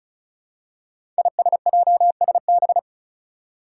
Morse code, heard as a keyed tone, about 34 wpm, 700 Hz. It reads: ISJSB
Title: Five Challenging Letter-Number Combinations 35wpm